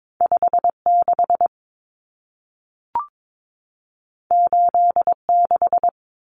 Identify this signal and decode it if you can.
Morse code, heard as a keyed tone, about 22 wpm, 700 Hz. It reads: 56 86
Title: Common Cut Numbers - 2-Digit 22wpm